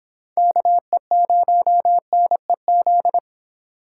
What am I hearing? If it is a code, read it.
Morse code, heard as a keyed tone, about 30 wpm, 700 Hz. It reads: KE0NEZ